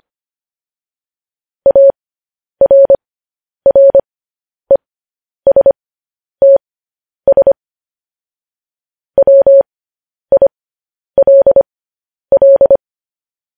Morse code, heard as a keyed tone, about 25 wpm, spaced as for 10 wpm, 550 Hz. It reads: ARRESTS WILL